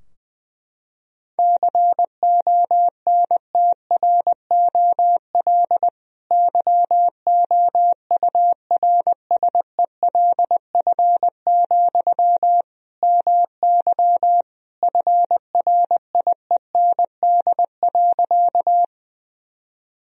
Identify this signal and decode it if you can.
Morse code, heard as a keyed tone, about 21 wpm, 700 Hz. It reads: CONTROL YOURSELF, MY FRIEND.
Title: Sentences from Top 1000 Words - Review Part 3 - 20wpm